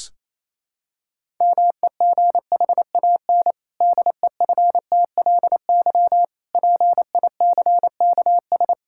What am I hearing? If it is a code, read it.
Morse code, heard as a keyed tone, about 28 wpm, 700 Hz. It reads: MEGHAN DEFTLY PICKS
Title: Pangram Challenge v2 - 28wpm